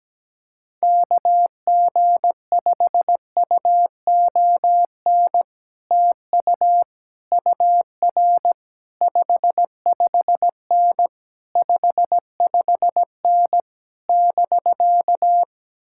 Morse code, heard as a keyed tone, about 17 wpm, 700 Hz. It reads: KG5UON TU UR 55N 55N <BK>
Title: Speed-Racing Common QSO Phrases - 17wpm